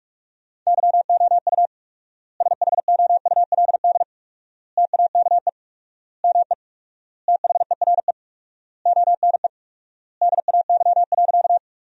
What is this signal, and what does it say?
Morse code, heard as a keyed tone, about 44 wpm, 700 Hz. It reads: YOU SHOULD TAKE ME THERE ONE DAY.
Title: 4x Word-Spacing - Sentences from Top 300 Words - 45wpm